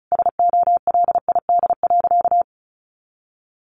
Morse code, heard as a keyed tone, about 34 wpm, 700 Hz. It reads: SOLID.
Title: Sentences from Top 1500 Words 35wpm